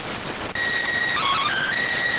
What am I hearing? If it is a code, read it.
SSTV (slow-scan television), PD 290.